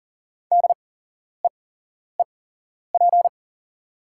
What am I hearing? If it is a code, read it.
Morse code, heard as a keyed tone, about 40 wpm, spaced as for 11 wpm, 690 Hz. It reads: DEEP
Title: 8x Character Spacing - Top 500 Words - 40wpm